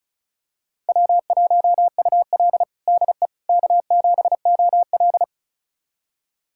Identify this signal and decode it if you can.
Morse code, heard as a keyed tone, about 34 wpm, 700 Hz. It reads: W1UL DE K7OL
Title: USA Callsign DE Callsign - 4 Characters - 35wpm